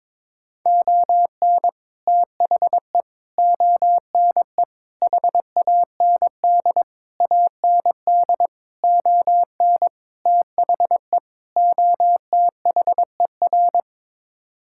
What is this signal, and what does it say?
Morse code, heard as a keyed tone, about 22 wpm, 700 Hz. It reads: ON THE ONE HAND AND ON THE OTHER
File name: practice / 22wpm / Sets of 8 Words - 22wpm